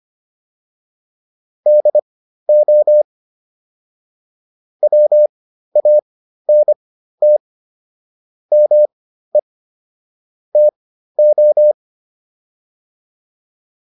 Morse code, heard as a keyed tone, about 25 wpm, spaced as for 13 wpm, 600 Hz. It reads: DO WANT ME TO